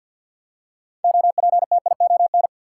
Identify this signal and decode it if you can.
Morse code, heard as a keyed tone, about 53 wpm, 700 Hz. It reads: OPTION
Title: Six Letter Words - 50wpm